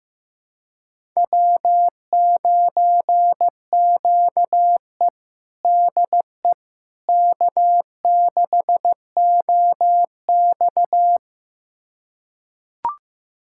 Morse code, heard as a keyed tone, about 15 wpm, 700 Hz. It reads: W9QE DE K6OX
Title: USA Callsign DE Callsign - 4 Characters - 15wpm